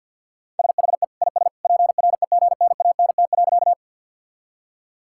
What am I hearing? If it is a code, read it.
Morse code, heard as a keyed tone, about 51 wpm, 700 Hz. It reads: SHE IS PREGNANT.
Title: Sentences from Top 1000 Words - Review Part 6 - 50wpm